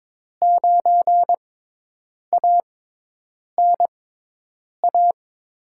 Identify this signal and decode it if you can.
Morse code, heard as a keyed tone, about 22 wpm, spaced as for 8 wpm, 700 Hz. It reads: 9ANA
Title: USA Call Signs - 5 Characters - 6x Character Spacing - 22wpm